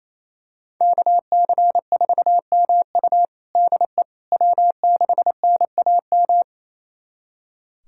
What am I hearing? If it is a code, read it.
Morse code, heard as a keyed tone, about 28 wpm, 700 Hz. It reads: KC4MU DE W6NAM
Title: USA Callsign DE Callsign - 5 Characters - 28wpm